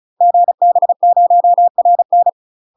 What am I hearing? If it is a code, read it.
Morse code, heard as a keyed tone, about 36 wpm, 700 Hz. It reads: GB0RN
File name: UK MASTER QSOs TRAINING / 35WPM / UK MASTER QSOs 0023